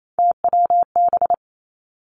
Morse code, heard as a keyed tone, about 28 wpm, 700 Hz. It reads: TWB
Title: Three Challenging Letter-Number Combinations 28wpm